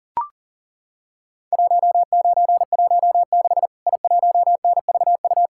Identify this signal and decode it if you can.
Morse code, heard as a keyed tone, about 40 wpm, 700 Hz. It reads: 1916 I1NVU
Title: Worked All Europe DX Contest - QTC Exchange - 40wpm